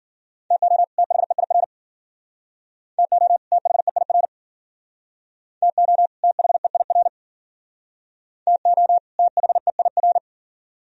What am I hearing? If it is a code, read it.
Morse code, heard as a keyed tone, about 53 wpm, 700 Hz. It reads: TO THEIR TO THEIR TO THEIR TO THEIR